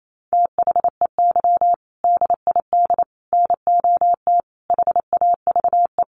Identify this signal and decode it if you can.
Morse code, heard as a keyed tone, about 28 wpm, 700 Hz. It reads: THEY DID NOT HAVE
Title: Sentences from Top 1400 Words 28wpm